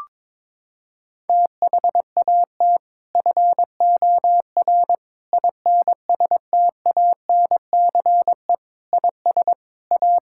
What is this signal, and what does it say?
Morse code, heard as a keyed tone, about 21 wpm, 700 Hz. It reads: THAT FOR INSTANCE IS A
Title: Sentences from Top 1000 Words - Review Part 5 - 22wpm